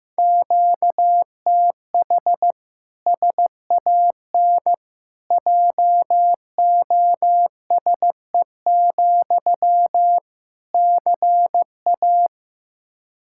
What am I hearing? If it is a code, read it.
Morse code, heard as a keyed tone, about 16 wpm, 700 Hz. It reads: QTH SAN JOSE, CA
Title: Practice QSO - Exchange 1 - Beginning - 15wpm